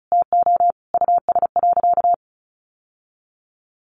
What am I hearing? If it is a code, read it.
Morse code, heard as a keyed tone, about 34 wpm, 700 Hz. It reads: TO US.
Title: Sentences from Top 1200 Words 35wpm